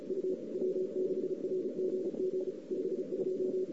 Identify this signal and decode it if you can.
Morse code, heard as a keyed tone, about 35 wpm, 410 Hz. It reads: U W6RGG 52
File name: CW / 2008-ss-cw-k5tr-op-n5rz -08:00